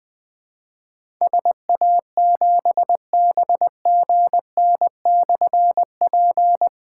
Morse code, heard as a keyed tone, about 20 wpm, 700 Hz. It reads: SA7BGN/P
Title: Difficult Call Signs 20wpm